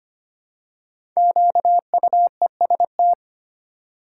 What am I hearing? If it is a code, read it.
Morse code, heard as a keyed tone, about 25 wpm, 700 Hz. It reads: QUEST
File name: 5 Letter Words - First 3 Letters Given - 25wpm